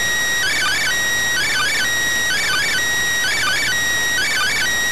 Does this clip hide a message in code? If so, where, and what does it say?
Yes: Emergency Alert System (SAME) tones from 0.4 to 4.6 s, heard as harsh buzzing bursts.